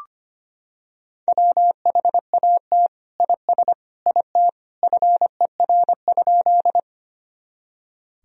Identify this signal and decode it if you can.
Morse code, heard as a keyed tone, about 25 wpm, 700 Hz. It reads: WHAT IS IT FER?
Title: Sentences from Top 100 Words and Abbreviations 25wpm